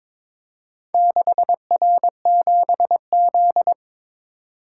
Morse code, heard as a keyed tone, about 22 wpm, 700 Hz. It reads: 6R7Z
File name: practice / 22wpm / Four Letter-Number Combinations - 22wpm